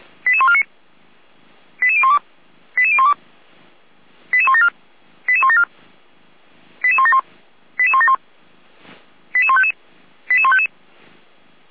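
Five-tone selective call (ZVEI), heard as quick runs of five tones, tones of about 70 ms. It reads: 88149 88012 88012 88174 88174 88171 88171 88155 88155
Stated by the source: ZVEI five-tone call received by SDR, server of Feuerwehr Orlamünde fire brigade.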